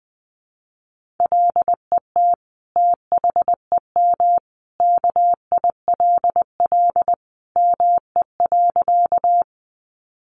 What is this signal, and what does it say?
Morse code, heard as a keyed tone, about 20 wpm, 700 Hz. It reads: LET THEM KILL ME.
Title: Sentences from Top 1400 Words 20wpm